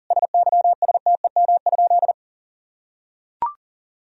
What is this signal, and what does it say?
Morse code, heard as a keyed tone, about 40 wpm, 700 Hz. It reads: SYSTEM?
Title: Sentences from Top 1000 Words - Review Part 3 - 40wpm